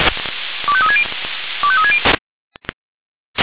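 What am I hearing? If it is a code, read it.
Five-tone selective call (ZVEI), heard as quick runs of five tones, tones of about 70 ms. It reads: 25360 25360
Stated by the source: ZVEI five-tone selective call received on an SDR.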